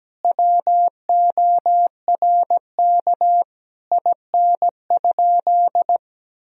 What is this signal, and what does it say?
Morse code, heard as a keyed tone, about 17 wpm, 700 Hz. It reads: WORK IN?
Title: Sentences from Top 1000 Words - Review Part 4 - 17wpm